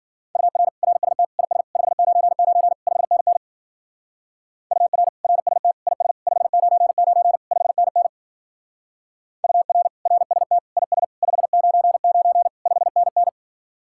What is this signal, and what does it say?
Morse code, heard as a keyed tone, about 48 wpm, 700 Hz. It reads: UR RST IS 599 5NN UR RST IS 599 5NN UR RST IS 599 5NN
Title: Speed-Racing Common QSO Phrases - 40wpm